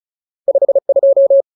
Morse code, heard as a keyed tone, about 34 wpm, 550 Hz. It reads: 52
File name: Ultimate Word List / 35WPM Ultimate